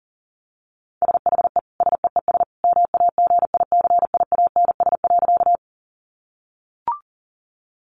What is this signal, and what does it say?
Morse code, heard as a keyed tone, about 40 wpm, 700 Hz. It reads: SHE SEES MAGICIANS.
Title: Easy Phrases 40wpm